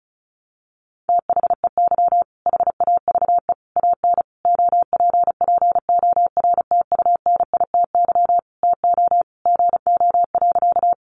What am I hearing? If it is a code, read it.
Morse code, heard as a keyed tone, about 34 wpm, 700 Hz. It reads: THEY HAVE AN OPPORTUNITY TO GO.
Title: Sentences from Top 1000 Words 35wpm